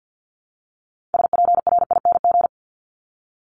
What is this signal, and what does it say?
Morse code, heard as a keyed tone, about 50 wpm, 700 Hz. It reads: SPRING
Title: Top 1001-1250 Words 50wpm